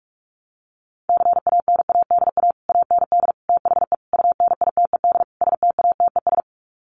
Morse code, heard as a keyed tone, about 44 wpm, 700 Hz. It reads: CANADA AND THE UNITED STATES
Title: Sets of 5 Words Encore 45wpm